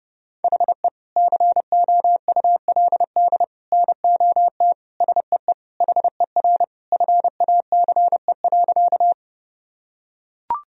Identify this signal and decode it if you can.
Morse code, heard as a keyed tone, about 30 wpm, 700 Hz. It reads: HE COULD NOT SEE HER FACE.
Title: Sentences from Top 1000 Words - Review Part 2 - 30wpm